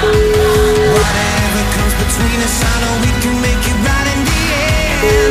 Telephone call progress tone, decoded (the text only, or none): ringback tone